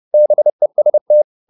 Morse code, heard as a keyed tone, about 30 wpm, 600 Hz. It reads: BEST